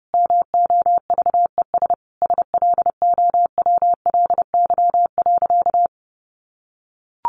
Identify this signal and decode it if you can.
Morse code, heard as a keyed tone, about 30 wpm, 700 Hz. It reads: MOVES SLOWLY.
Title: Easy Phrases 30wpm